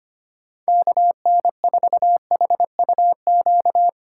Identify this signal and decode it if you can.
Morse code, heard as a keyed tone, about 25 wpm, 700 Hz. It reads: KN4HUQ
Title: USA Call Signs - 6 Characters - 3x Character Spacing - 25wpm